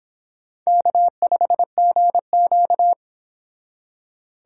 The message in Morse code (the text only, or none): K5GQ